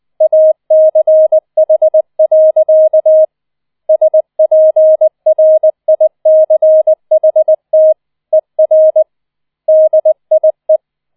Morse code, heard as a keyed tone, about 20 wpm, 630 Hz. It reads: ACH. SPRICHT ER DIE